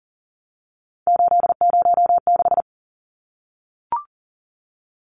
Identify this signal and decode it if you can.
Morse code, heard as a keyed tone, about 38 wpm, 700 Hz. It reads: 806
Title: Numbers - 3-Digit 40wpm